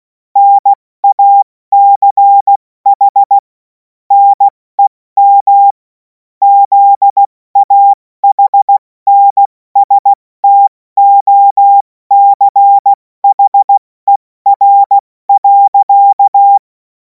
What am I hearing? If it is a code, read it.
Morse code, heard as a keyed tone, about 16 wpm, spaced as for 14 wpm, 800 Hz. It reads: NACH NEM ZAHNSTOCHER.
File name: Morsekurs / Witze / Witz 17. März 2026 70 F80